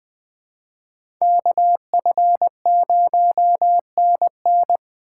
Morse code, heard as a keyed tone, about 20 wpm, 700 Hz. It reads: KF0NN